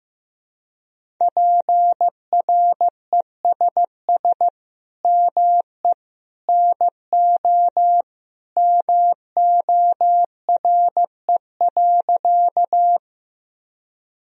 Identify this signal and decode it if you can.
Morse code, heard as a keyed tone, about 15 wpm, 700 Hz. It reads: PRESS ME NO MORE.
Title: Sentences from Top 1000 Words - Review Part 1 - 15wpm